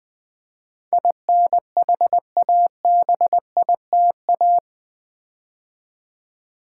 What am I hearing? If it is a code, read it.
Morse code, heard as a keyed tone, about 20 wpm, 700 Hz. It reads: INHABITA
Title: Build A Word - 4 Letters At A Time - 10-Letter Words - 20wpm